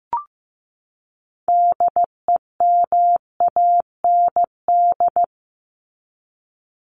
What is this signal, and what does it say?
Morse code, heard as a keyed tone, about 15 wpm, 700 Hz. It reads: DEMAND
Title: Top 501-750 Words 15wpm